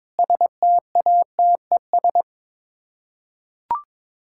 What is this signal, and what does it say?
Morse code, heard as a keyed tone, about 22 wpm, 700 Hz. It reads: STATES
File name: practice / 22wpm / Sets of 8 Words - 22wpm